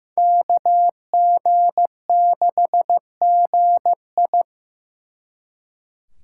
Morse code, heard as a keyed tone, about 17 wpm, 700 Hz. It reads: KG6GI
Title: Speed Racing - USA 2x2 Call Signs 15wpm